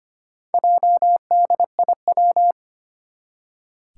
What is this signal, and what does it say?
Morse code, heard as a keyed tone, about 25 wpm, 700 Hz. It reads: JDIW
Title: Four Challenging Letter-Number Combinations 25wpm